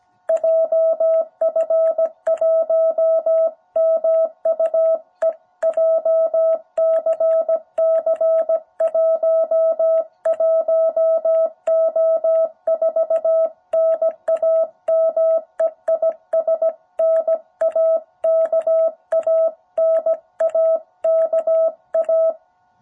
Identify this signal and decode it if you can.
Morse code, heard as a keyed tone, about 17 wpm, 660 Hz. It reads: JF1MUE JCC 11O4 NAME IS NAKA NAKA
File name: jh8cbh / 2021cw